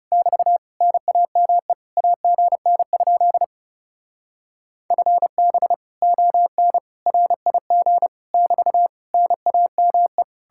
Morse code, heard as a keyed tone, about 36 wpm, 700 Hz. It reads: <BT> NAME AGN? FB ON RIG <BT> NAME
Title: Speed-Racing Common QSO Phrases - 30wpm